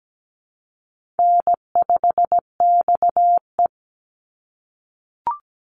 Morse code, heard as a keyed tone, about 17 wpm, 700 Hz. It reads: N5XE
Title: Easy Call Signs - 17wpm